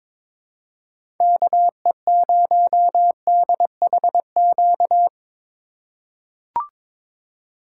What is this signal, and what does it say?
Morse code, heard as a keyed tone, about 22 wpm, 700 Hz. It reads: KE0DHQ
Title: USA Call Signs - 6 Characters - 3x Character Spacing - 22wpm